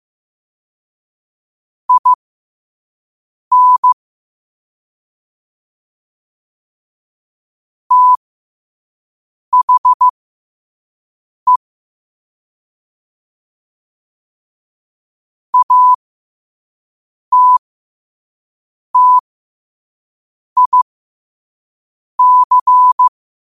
Morse code, heard as a keyed tone, about 15 wpm, spaced as for 5 wpm, 1000 Hz. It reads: IN THE ATTIC
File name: qotdfiles / 5wpm